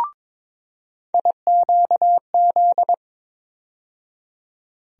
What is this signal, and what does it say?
Morse code, heard as a keyed tone, about 22 wpm, 700 Hz. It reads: IQZ
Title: USA Call Sign Postfixes - 3 Letters 22wpm